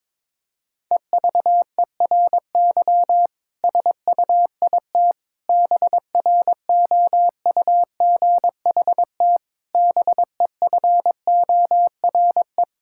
Morse code, heard as a keyed tone, about 22 wpm, 700 Hz. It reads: EVERY SUIT BROUGHT BEFORE
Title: Sentences from Top 1600 Words 22wpm